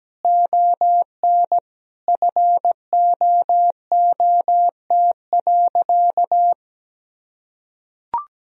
Morse code, heard as a keyed tone, about 17 wpm, 700 Hz. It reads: ON FOOT.
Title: Sentences from Top 1000 Words - Review Part 3 - 17wpm